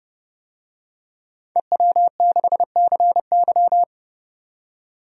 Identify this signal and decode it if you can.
Morse code, heard as a keyed tone, about 30 wpm, 700 Hz. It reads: EW6CY
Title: DXCC-100 Call Signs - 5 Characters - 30wpm